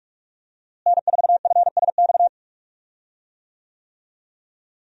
Morse code, heard as a keyed tone, about 46 wpm, 700 Hz. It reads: N4FSX
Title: USA Call Signs - 5 Characters - Repeated 3x - 45wpm